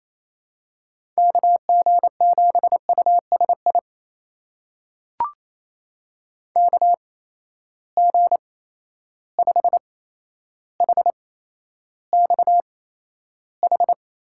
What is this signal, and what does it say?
Morse code, heard as a keyed tone, about 28 wpm, spaced as for 12 wpm, 700 Hz. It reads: KG7USI KG5HXH
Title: USA Call Signs - 8x Character Spacing - 28wpm